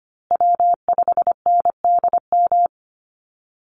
Morse code, heard as a keyed tone, about 25 wpm, 700 Hz. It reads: W5NDM
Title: USA Call Signs Repeated 2x 25wpm